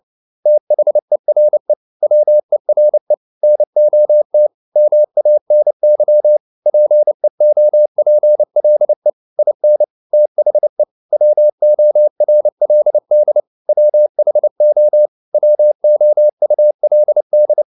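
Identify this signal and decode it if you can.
Morse code, heard as a keyed tone, about 29 wpm, 600 Hz. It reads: THERE WERE NOT MANY PEOPLE IN THE WORLD WHO WOULD
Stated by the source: Morse code practice at 28 wpm, text of Ken Follett's novel Never in English, from CW med Gnister.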